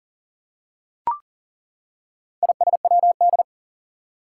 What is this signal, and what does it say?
Morse code, heard as a keyed tone, about 39 wpm, 700 Hz. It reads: ISWD
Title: Four Challenging Letter-Number Combinations 40wpm